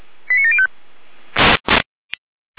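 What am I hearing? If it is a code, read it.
Five-tone selective call (ZVEI), heard as quick runs of five tones, tones of about 70 ms. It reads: 89705
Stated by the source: ZVEI five-tone selective call 89740 received on an SDR.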